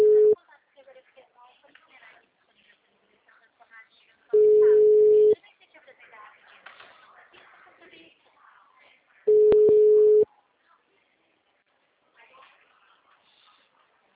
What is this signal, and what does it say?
Ringback tone (Europe, 425 Hz), with speech.